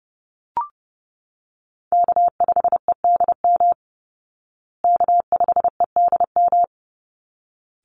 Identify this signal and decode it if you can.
Morse code, heard as a keyed tone, about 30 wpm, 700 Hz. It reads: K5EDM K5EDM
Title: Call Signs Repeated 3x - 30wpm